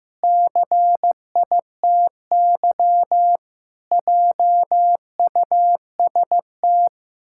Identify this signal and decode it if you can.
Morse code, heard as a keyed tone, about 15 wpm, 700 Hz. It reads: CITY JUST
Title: Pangram Challenge v2 - 15wpm